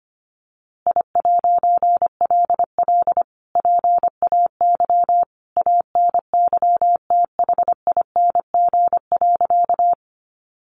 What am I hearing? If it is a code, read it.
Morse code, heard as a keyed tone, about 25 wpm, 700 Hz. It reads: I'LL PAY ANYTHING.
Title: A Sound Of Thunder 25wpm